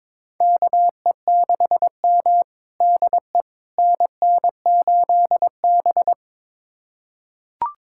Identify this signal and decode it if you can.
Morse code, heard as a keyed tone, about 22 wpm, 700 Hz. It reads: KE6M DE NN8B
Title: USA Callsign DE Callsign - 4 Characters - 22wpm